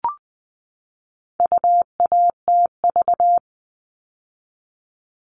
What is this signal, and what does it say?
Morse code, heard as a keyed tone, about 20 wpm, 700 Hz. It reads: UATV
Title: All Cut Numbers - 4-Digit 20wpm